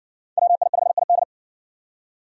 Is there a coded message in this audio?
Yes: Morse code from 0.4 to 1.2 s, heard as a keyed tone, about 60 wpm, 700 Hz. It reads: WI5ID